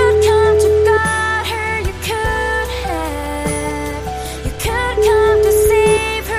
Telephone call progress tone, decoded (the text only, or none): ringback tone